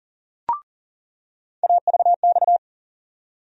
Morse code, heard as a keyed tone, about 40 wpm, 700 Hz. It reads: AVX
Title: USA Call Sign Postfixes - 3 Letters 40wpm